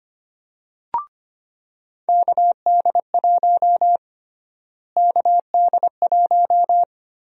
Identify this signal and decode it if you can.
Morse code, heard as a keyed tone, about 25 wpm, 700 Hz. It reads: KD1 KD1
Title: USA Call Sign Prefixes - Repeated 2x - 25wpm